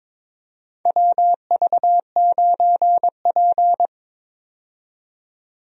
Morse code, heard as a keyed tone, about 22 wpm, 700 Hz. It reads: WV9P
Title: USA Call Signs - 4 Characters - 22wpm